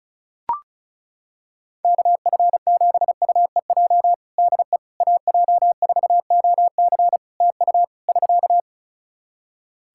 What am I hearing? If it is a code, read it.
Morse code, heard as a keyed tone, about 34 wpm, 700 Hz. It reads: KF7UEJ DE AJ4OC TU <SK>
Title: Practice QSO - Exchange 4 - Ending - 35wpm